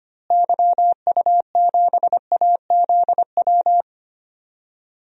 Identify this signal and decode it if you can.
Morse code, heard as a keyed tone, about 25 wpm, 700 Hz. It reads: YU7AZW
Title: DXCC-100 Call Signs - 6 Characters - Repeated 3x - 25wpm